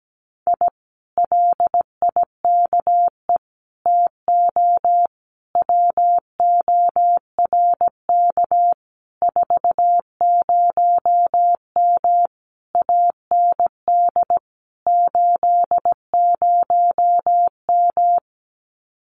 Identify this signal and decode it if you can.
Morse code, heard as a keyed tone, about 17 wpm, 700 Hz. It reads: I LIKE TO WORK 40M AND 80M
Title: Common QSO Phrases Encore 17wpm